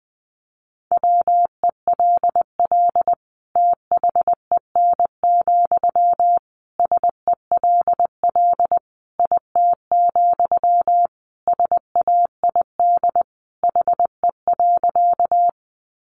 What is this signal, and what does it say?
Morse code, heard as a keyed tone, about 20 wpm, 700 Hz. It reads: WELL THEN, SELL IT, SAID HE.
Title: Sentences from Top 1100 Words 20wpm